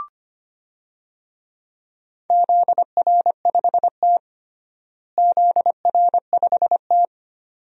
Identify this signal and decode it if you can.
Morse code, heard as a keyed tone, about 25 wpm, 700 Hz. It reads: ZR5T ZR5T
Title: DXCC-100 Call Signs - 4 Characters - Repeated 2x - 25wpm